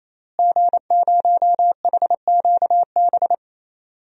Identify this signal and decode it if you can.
Morse code, heard as a keyed tone, about 27 wpm, 700 Hz. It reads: G0HQB
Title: DXCC-100 Call Signs - 5 Characters - Repeated 3x - 28wpm